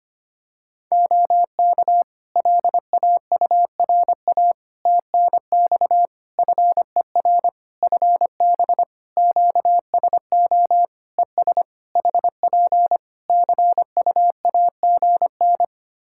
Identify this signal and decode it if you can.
Morse code, heard as a keyed tone, about 25 wpm, 700 Hz. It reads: OK LAURA TNX FER FB QSO ES HP CUAGN
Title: K1USN SST QSO Practice 25wpm